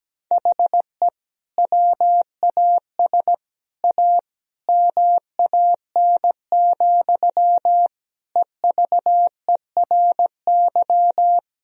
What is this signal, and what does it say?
Morse code, heard as a keyed tone, about 17 wpm, 700 Hz. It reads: HE WAS A MAN, EVERY
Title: Sentences from Top 1000 Words - Review Part 3 - 17wpm